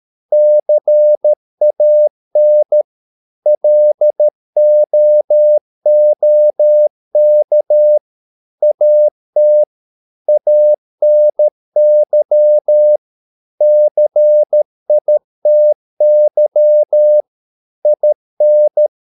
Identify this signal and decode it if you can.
Morse code, heard as a keyed tone, about 13 wpm, 600 Hz. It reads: CAN LOOK AT ANY CITY IN